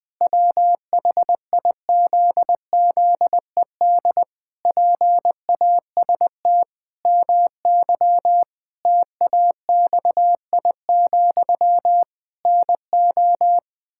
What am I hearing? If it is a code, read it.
Morse code, heard as a keyed tone, about 20 wpm, 700 Hz. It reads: WHIZZED PAST MY TAXI, NO
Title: Pangram Challenge v2 - 20wpm